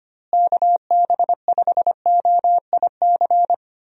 Morse code, heard as a keyed tone, about 25 wpm, 700 Hz. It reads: KB5OIC
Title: USA Call Signs - 6 Characters - Repeated 2x - 25wpm